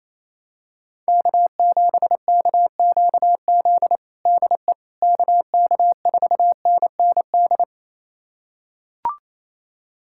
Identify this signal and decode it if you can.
Morse code, heard as a keyed tone, about 28 wpm, 700 Hz. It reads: K7KQZ DE KK4NND
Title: USA Callsign DE Callsign 28wpm